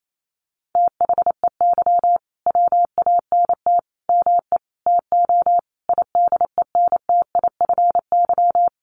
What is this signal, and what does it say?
Morse code, heard as a keyed tone, about 28 wpm, 690 Hz. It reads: THEY WANT ME TO IDENTIFY
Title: Sentences from Top 900 Words 28wpm